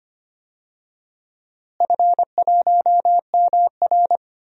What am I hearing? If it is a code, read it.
Morse code, heard as a keyed tone, about 25 wpm, 700 Hz. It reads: F1MR
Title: DXCC-100 Call Signs - 4 Characters - Repeated 2x - 25wpm